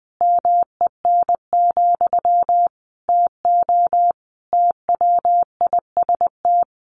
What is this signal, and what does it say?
Morse code, heard as a keyed tone, about 20 wpm, 700 Hz. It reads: MEN, TO TWIST
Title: A Sound Of Thunder 20wpm